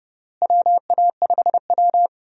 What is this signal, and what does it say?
Morse code, heard as a keyed tone, about 32 wpm, 700 Hz. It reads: WA5W
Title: USA Call Signs - 4 Characters - Repeated 2x - 30wpm